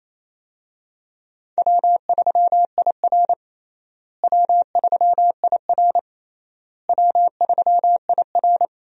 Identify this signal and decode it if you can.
Morse code, heard as a keyed tone, about 28 wpm, 700 Hz. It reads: W3IR W3IR W3IR
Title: USA Call Signs - 4 Characters - Repeated 3x - 28wpm